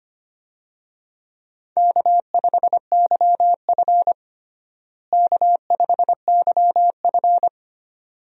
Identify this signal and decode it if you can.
Morse code, heard as a keyed tone, about 25 wpm, 700 Hz. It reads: K5YF K5YF
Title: USA Call Signs - 4 Characters - Repeated 2x - 25wpm